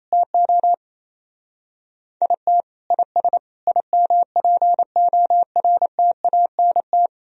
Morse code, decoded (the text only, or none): TO IT IS IMPORTANT